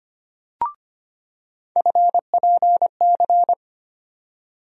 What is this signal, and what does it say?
Morse code, heard as a keyed tone, about 25 wpm, 700 Hz. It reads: FPC
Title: USA Call Sign Postfixes - 3 Letters 25wpm